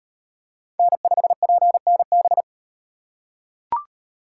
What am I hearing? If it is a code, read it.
Morse code, heard as a keyed tone, about 38 wpm, 700 Hz. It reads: N5PNB